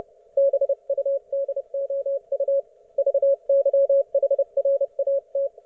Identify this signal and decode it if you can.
Morse code, heard as a keyed tone, about 30 wpm, 550 Hz. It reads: BUDOU VYHRAT
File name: Archiv / CUC181 / ok1fhi cul181